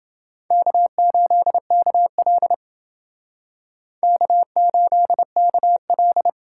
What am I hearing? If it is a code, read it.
Morse code, heard as a keyed tone, about 29 wpm, 700 Hz. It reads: K8KL K8KL